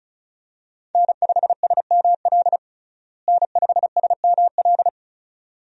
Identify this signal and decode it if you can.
Morse code, heard as a keyed tone, about 34 wpm, 700 Hz. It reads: N5SML N5SML
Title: USA Call Signs - 5 Characters - Repeated 2x - 35wpm